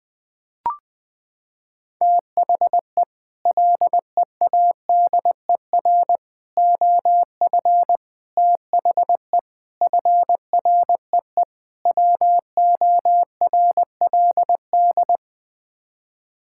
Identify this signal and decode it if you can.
Morse code, heard as a keyed tone, about 20 wpm, 700 Hz. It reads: THE LEADER OF THE FREE WORLD
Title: Sets of 6 Words - 20wpm